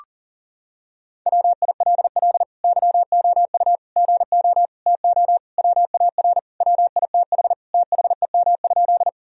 Morse code, heard as a keyed tone, about 40 wpm, 700 Hz. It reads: WILL YOU GO TO WAR WITH THEM?
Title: Sentences from Top 1000 Words - Review Part 2 - 40wpm